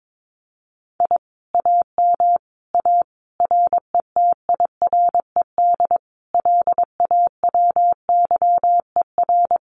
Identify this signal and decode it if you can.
Morse code, heard as a keyed tone, about 22 wpm, 700 Hz. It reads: I AM A RETIRED LAWYER
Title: Common QSO Phrases Encore 22wpm